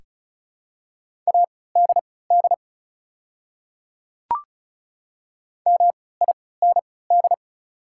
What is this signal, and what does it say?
Morse code, heard as a keyed tone, about 34 wpm, spaced as for 20 wpm, 690 Hz. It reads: ADD MIND
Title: 3x Character Spacing - Top 500 Words - 35wpm